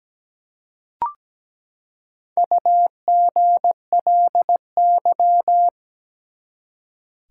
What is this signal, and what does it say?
Morse code, heard as a keyed tone, about 17 wpm, 700 Hz. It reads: UGLY
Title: Four Letter Words - 17wpm